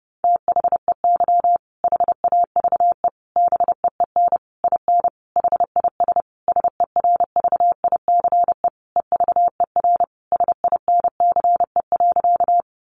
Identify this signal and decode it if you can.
Morse code, heard as a keyed tone, about 30 wpm, 700 Hz. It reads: THEY HAVE BEEN IN HIS SERVICE EVER SINCE.
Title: Sentences from Top 700 Words 30wpm